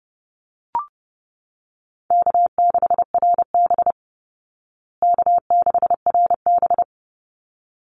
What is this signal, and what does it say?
Morse code, heard as a keyed tone, about 30 wpm, 700 Hz. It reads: K6RB K6RB
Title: Easy Call Signs Repeated 2x - 30wpm